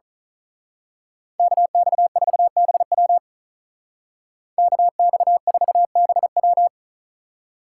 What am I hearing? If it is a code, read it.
Morse code, heard as a keyed tone, about 39 wpm, 700 Hz. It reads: KX4BW KX4BW
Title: Speed Racing - USA Call Signs 35wpm